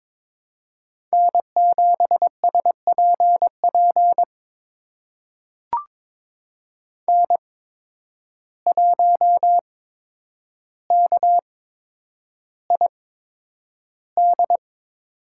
Morse code, heard as a keyed tone, about 22 wpm, spaced as for 9 wpm, 700 Hz. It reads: N7SPP N1KID
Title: USA Call Signs - 5 Characters - 8x Character Spacing - 22wpm